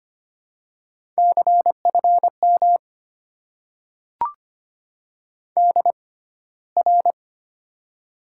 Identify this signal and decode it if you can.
Morse code, heard as a keyed tone, about 25 wpm, spaced as for 15 wpm, 700 Hz. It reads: CFM DR
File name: Top 90 QSO Elements - 6x Character Spacing - 25wpm